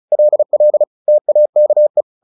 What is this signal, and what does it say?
Morse code, heard as a keyed tone, about 34 wpm, 600 Hz. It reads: LL TAKE
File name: the old man and the sea / the old man and the sea 1 35wpm